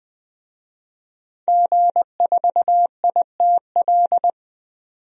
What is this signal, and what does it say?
Morse code, heard as a keyed tone, about 20 wpm, 700 Hz. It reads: G4ITL
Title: DXCC-100 Call Signs - 5 Characters - 20wpm